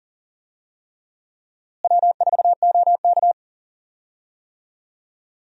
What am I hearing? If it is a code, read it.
Morse code, heard as a keyed tone, about 40 wpm, 700 Hz. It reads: W4OK
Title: USA Call Signs - 4 Characters - Repeated 3x - 40wpm